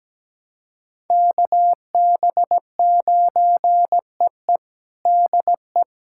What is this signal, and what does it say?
Morse code, heard as a keyed tone, about 17 wpm, 700 Hz. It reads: KB9EE DE
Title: USA Callsign DE Callsign 17wpm